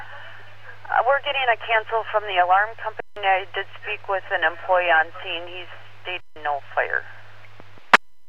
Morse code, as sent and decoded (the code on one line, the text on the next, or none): none